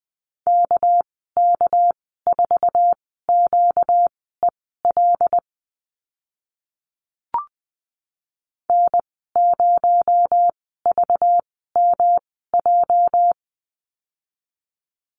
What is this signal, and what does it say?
Morse code, heard as a keyed tone, about 20 wpm, spaced as for 14 wpm, 700 Hz. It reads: KK4QEL N0VMJ
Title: 245 Lesson 39g - Letter z - Call Signs 20wpm